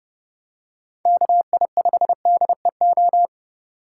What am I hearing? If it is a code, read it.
Morse code, heard as a keyed tone, about 30 wpm, 700 Hz. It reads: KI5DEO